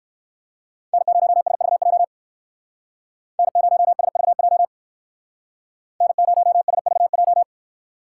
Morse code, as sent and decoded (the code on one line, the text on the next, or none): -. ----- ... ...- .--- / -. ----- ... ...- .--- / -. ----- ... ...- .---
N0SVJ N0SVJ N0SVJ